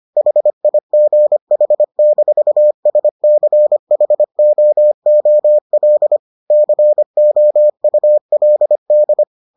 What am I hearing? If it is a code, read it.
Morse code, heard as a keyed tone, about 25 wpm, 600 Hz. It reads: HIGH-SCHOOL COULD